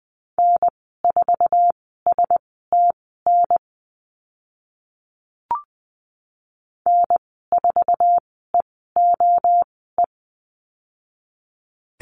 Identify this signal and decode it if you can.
Morse code, heard as a keyed tone, about 20 wpm, spaced as for 14 wpm, 700 Hz. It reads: N4STN N4EOE
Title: 062 Lesson 11g - Letter h - Call Signs 20wpm